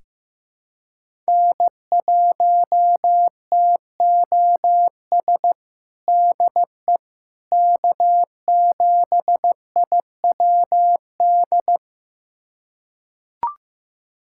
Morse code, heard as a keyed tone, about 15 wpm, 700 Hz. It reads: N1TOS DE K7IWD
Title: USA Callsign DE Callsign - 5 Characters - 15wpm